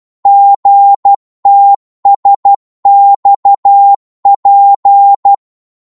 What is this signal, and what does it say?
Morse code, heard as a keyed tone, about 12 wpm, 800 Hz. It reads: GTSXP
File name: Morse / ltr-12wpm